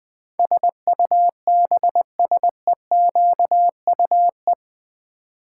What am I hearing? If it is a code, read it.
Morse code, heard as a keyed tone, about 20 wpm, 700 Hz. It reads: SUBSEQUE
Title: Build A Word - 4 Letters At A Time - 12-Letter Words - 20wpm